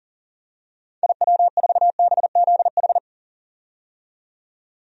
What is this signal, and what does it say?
Morse code, heard as a keyed tone, about 37 wpm, 700 Hz. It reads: IW4BZH
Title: DXCC-100 Call Signs - 6 Characters - 40wpm